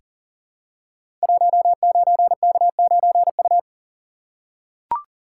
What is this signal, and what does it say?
Morse code, heard as a keyed tone, about 41 wpm, 700 Hz. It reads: 19K9U
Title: Code Groups - 5 Letter:Numbers - 40wpm